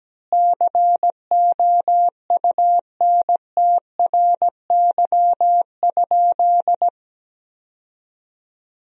Morse code, heard as a keyed tone, about 17 wpm, 700 Hz. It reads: COUNTRY?
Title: 2x Word-Spacing - Sentences from Top 300 Words - 17wpm